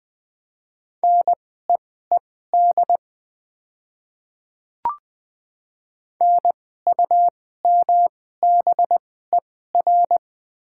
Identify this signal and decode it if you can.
Morse code, heard as a keyed tone, about 20 wpm, spaced as for 15 wpm, 700 Hz. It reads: NEED NUMBER
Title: 2x Character Spacing - Top 200 Words - 20wpm